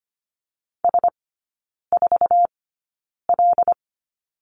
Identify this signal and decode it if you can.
Morse code, heard as a keyed tone, about 25 wpm, spaced as for 9 wpm, 700 Hz. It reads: S4L
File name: course / 25:10wpm / 093 Lesson 16e - Letter c - Three Characters 25wpm